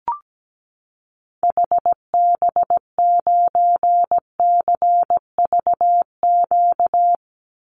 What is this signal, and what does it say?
Morse code, heard as a keyed tone, about 17 wpm, 700 Hz. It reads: HB9CVQ
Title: Call Signs Repeated 2x - 17wpm